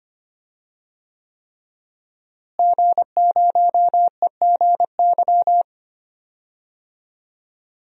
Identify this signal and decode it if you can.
Morse code, heard as a keyed tone, about 25 wpm, 700 Hz. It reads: G0EGY
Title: DXCC-100 Call Signs - 5 Characters - Repeated 3x - 25wpm